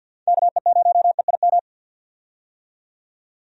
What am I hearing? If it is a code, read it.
Morse code, heard as a keyed tone, about 50 wpm, 700 Hz. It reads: KE0EIM
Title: USA Call Signs - 6 Characters - 50wpm